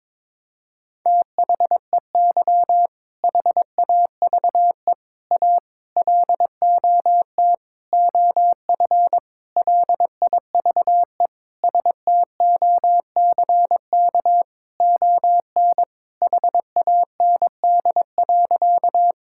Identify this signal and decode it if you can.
Morse code, heard as a keyed tone, about 21 wpm, 700 Hz. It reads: THEY HAVE A LOT OF LIVE STOCK ON HAND.
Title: Sentences from Top 1000 Words - Review Part 5 - 22wpm